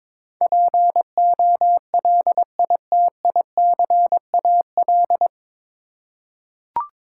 Morse code, heard as a keyed tone, about 21 wpm, 700 Hz. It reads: POLITICAL
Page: Morse Code Ninja practice 22wpm